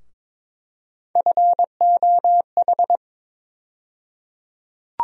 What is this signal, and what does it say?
Morse code, heard as a keyed tone, about 22 wpm, 700 Hz. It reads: FOH
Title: USA Call Sign Postfixes - 3 Letters 22wpm